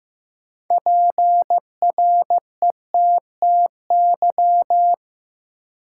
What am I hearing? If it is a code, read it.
Morse code, heard as a keyed tone, about 15 wpm, 700 Hz. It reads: PRETTY
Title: Sentences from Top 1000 Words - New Word Drill - 15wpm